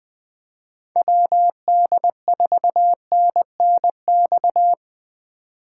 Morse code, heard as a keyed tone, about 20 wpm, 700 Hz. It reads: WD4NNX
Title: USA Call Signs - 6 Characters - 4x Character Spacing - 20wpm